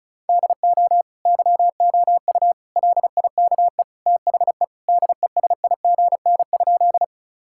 Morse code, heard as a keyed tone, about 36 wpm, 700 Hz. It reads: DO YOU LIKE THE DESIGN?
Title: Sentences from Top 1000 Words - Review Part 5 - 35wpm